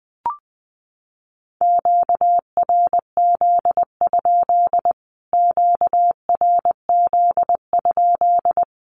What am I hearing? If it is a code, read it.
Morse code, heard as a keyed tone, about 20 wpm, 700 Hz. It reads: QRZ? QRZ?
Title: Q-Codes Repeated 3x 20wpm